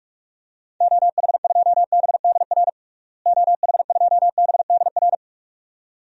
Morse code, heard as a keyed tone, about 46 wpm, 700 Hz. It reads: OH2BDR OH2BDR
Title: DXCC-100 Call Signs - 6 Characters - Repeated 2x - 45wpm